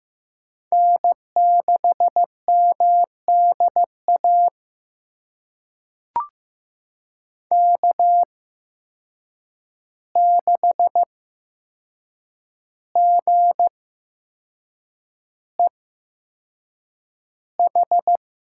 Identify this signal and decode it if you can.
Morse code, heard as a keyed tone, about 15 wpm, spaced as for 6 wpm, 700 Hz. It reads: N6MDA K6GEH
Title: USA Call Signs - 5 Characters - 8x Character Spacing - 15wpm